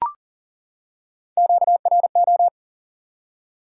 Morse code, heard as a keyed tone, about 40 wpm, 700 Hz. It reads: QRO
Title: Top 90 QSO Elements 40wpm